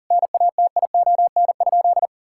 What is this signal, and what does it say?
Morse code, heard as a keyed tone, about 40 wpm, 700 Hz. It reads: NATION?
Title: Sentences from Top 1000 Words - Review Part 6 - 40wpm